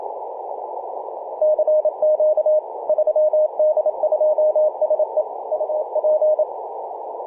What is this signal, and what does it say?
Morse code, heard as a keyed tone, about 27 wpm, 620 Hz, with radio noise. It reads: CQ 3D2SE UP
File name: 3d2se-18-080mhz-cw